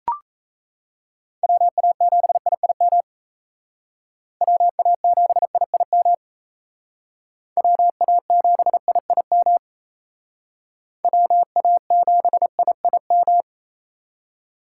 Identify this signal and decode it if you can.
Morse code, heard as a keyed tone, about 35 wpm, 700 Hz. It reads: WA7IIM WA7IIM WA7IIM WA7IIM
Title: Speed Racing - USA 2x3 Call Signs 28wpm